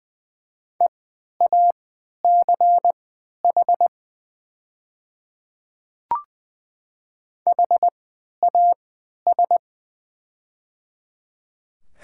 Morse code, heard as a keyed tone, about 20 wpm, spaced as for 11 wpm, 690 Hz. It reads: EACH HAS
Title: 3x Character Spacing - Top 100 Words - 20wpm